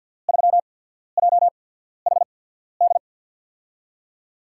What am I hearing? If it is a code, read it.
Morse code, heard as a keyed tone, about 47 wpm, spaced as for 13 wpm, 700 Hz. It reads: 3JHD
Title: USA Call Signs - 6 Characters - 8x Character Spacing - 50wpm